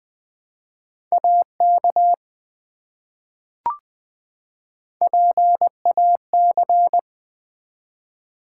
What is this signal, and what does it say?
Morse code, heard as a keyed tone, about 20 wpm, 700 Hz. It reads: AK PAC
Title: ARRL Sections v2 - 20wpm